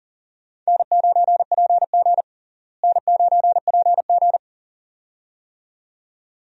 Morse code, heard as a keyed tone, about 40 wpm, 700 Hz. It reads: N9PG N9PG
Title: USA Call Signs - 4 Characters - Repeated 2x - 40wpm